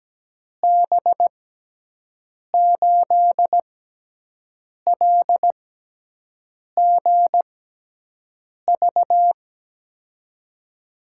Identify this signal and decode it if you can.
Morse code, heard as a keyed tone, about 17 wpm, spaced as for 6 wpm, 700 Hz. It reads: B8LGV
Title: USA Call Signs - 6 Characters - 6x Character Spacing - 17wpm